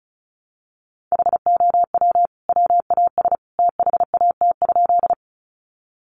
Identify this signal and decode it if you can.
Morse code, heard as a keyed tone, about 34 wpm, 700 Hz. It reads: HOW WAS THAT?
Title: Sentences from Top 100 Words 35wpm